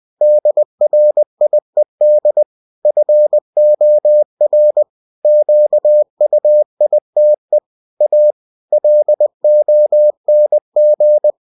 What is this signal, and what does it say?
Morse code, heard as a keyed tone, about 20 wpm, 600 Hz. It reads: DRIED FOR QUITE A LONG
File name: cw-books / winnie-the-pooh / winnie-20wpm-page-0107